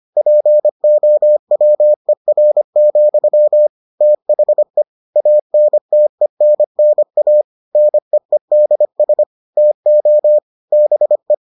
Morse code, heard as a keyed tone, about 25 wpm, 600 Hz. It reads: POWER, THE ANTENNA NEEDS TO BE